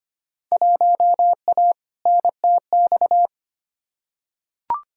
Morse code, heard as a keyed tone, about 25 wpm, 700 Hz. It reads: 1A NTX
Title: Field Day Exchanges - 25wpm